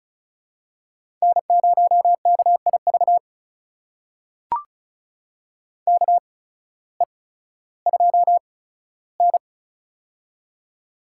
Morse code, heard as a keyed tone, about 36 wpm, spaced as for 16 wpm, 700 Hz. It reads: N0KIV KE2N
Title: USA Call Signs - 8x Character Spacing - 35wpm